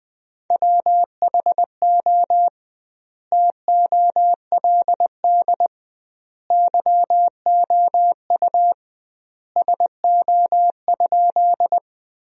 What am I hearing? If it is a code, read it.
Morse code, heard as a keyed tone, about 21 wpm, 700 Hz. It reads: WHO TOLD YOU SO?
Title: 2x Word-Spacing - Sentences from Top 400 Words - 20wpm